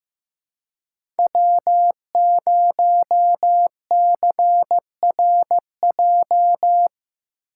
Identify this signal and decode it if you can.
Morse code, heard as a keyed tone, about 15 wpm, 700 Hz. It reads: W0CRJ
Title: USA Call Signs - 2x Character Spacing - 15wpm